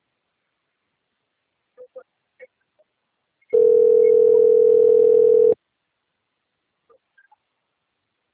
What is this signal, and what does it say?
Ringback tone (North America), 440 + 480 Hz.